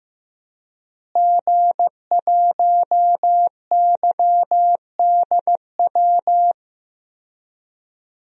Morse code, heard as a keyed tone, about 16 wpm, 700 Hz. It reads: G1YDW
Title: DXCC-100 Call Signs - 5 Characters - Repeated 2x - 15wpm